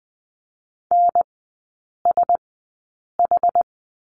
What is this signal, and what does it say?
Morse code, heard as a keyed tone, about 20 wpm, spaced as for 8 wpm, 700 Hz. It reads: NSH
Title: 067 Lesson 12e - Letter d - Three Characters 20wpm